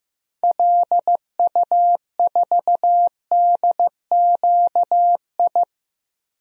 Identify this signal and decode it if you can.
Morse code, heard as a keyed tone, about 15 wpm, 700 Hz. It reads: LU4DQI
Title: DXCC-100 Call Signs - 6 Characters - 15wpm